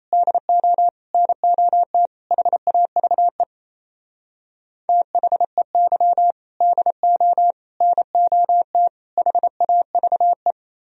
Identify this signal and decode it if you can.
Morse code, heard as a keyed tone, about 35 wpm, 700 Hz. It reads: DO NOT HAVE THEY DO NOT HAVE
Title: Speed-Racing - Sets of 4 Words 28wpm